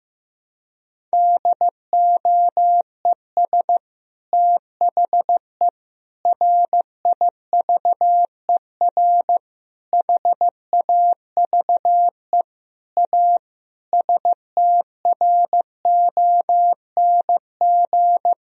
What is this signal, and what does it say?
Morse code, heard as a keyed tone, about 15 wpm, 700 Hz. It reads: DOES THE RIVER HAVE A STRONG
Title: Sentences from Top 1000 Words - Review Part 4 - 15wpm